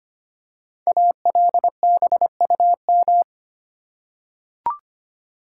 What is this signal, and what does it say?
Morse code, heard as a keyed tone, about 25 wpm, 700 Hz. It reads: ALBUM
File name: 5 Letter Words - First 3 Letters Given - 25wpm